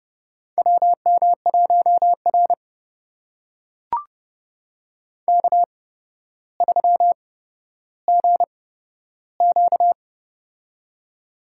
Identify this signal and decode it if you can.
Morse code, heard as a keyed tone, about 30 wpm, spaced as for 13 wpm, 700 Hz. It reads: WM1R K3GQ
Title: USA Call Signs - 4 Characters - 8x Character Spacing - 30wpm